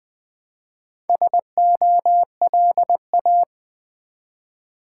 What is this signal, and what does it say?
Morse code, heard as a keyed tone, about 20 wpm, 700 Hz. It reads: SOLA
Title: Build A Word - 1 Letter At A Time - 5-Letter Words - 20wpm